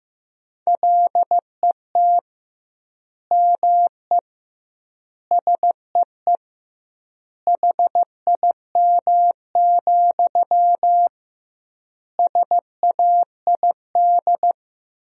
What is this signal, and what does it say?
Morse code, heard as a keyed tone, about 15 wpm, 700 Hz. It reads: LET ME SEE HIM, SAID
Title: 2x Word-Spacing - Sentences from Top 400 Words - 15wpm